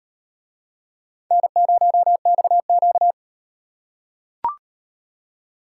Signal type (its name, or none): Morse code